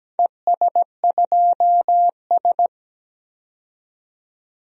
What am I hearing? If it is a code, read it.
Morse code, heard as a keyed tone, about 17 wpm, 700 Hz. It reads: ES2S
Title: DXCC-100 Call Signs - 4 Characters - 17wpm